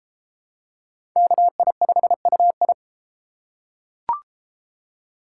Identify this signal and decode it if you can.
Morse code, heard as a keyed tone, about 30 wpm, 700 Hz. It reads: KI5UI